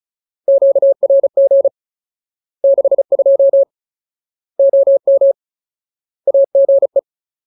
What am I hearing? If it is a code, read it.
Morse code, heard as a keyed tone, about 34 wpm, 550 Hz. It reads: QRG 62 OM AGE
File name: Ultimate Word List / 35WPM Ultimate